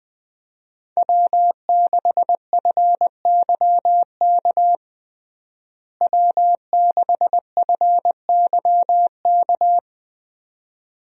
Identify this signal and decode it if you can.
Morse code, heard as a keyed tone, about 20 wpm, 700 Hz. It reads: W6FYK W6FYK
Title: USA Call Signs - 5 Characters - Repeated 2x - 20wpm